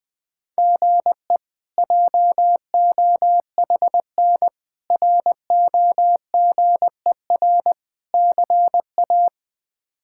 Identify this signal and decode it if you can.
Morse code, heard as a keyed tone, about 20 wpm, 700 Hz. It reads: GE JOHN ROGER CA
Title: K1USN SST QSO Practice - 20wpm